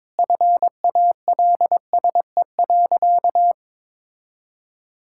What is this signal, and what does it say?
Morse code, heard as a keyed tone, about 22 wpm, 700 Hz. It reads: FALSE.
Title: Sentences from Top 1600 Words 22wpm